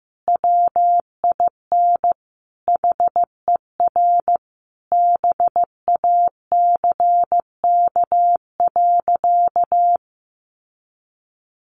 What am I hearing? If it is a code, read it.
Morse code, heard as a keyed tone, about 14 wpm, 700 Hz. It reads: WIN HER BACK.
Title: Sentences from Top 1500 Words 15wpm